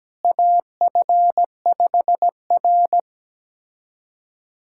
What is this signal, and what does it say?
Morse code, heard as a keyed tone, about 17 wpm, 700 Hz. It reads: AF5R
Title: USA Callsign DE Callsign - 4 Characters - 17wpm